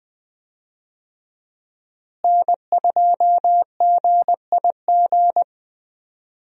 Morse code, heard as a keyed tone, about 20 wpm, 700 Hz. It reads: N2GIG
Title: USA Call Signs - 5 Characters - Repeated 3x - 20wpm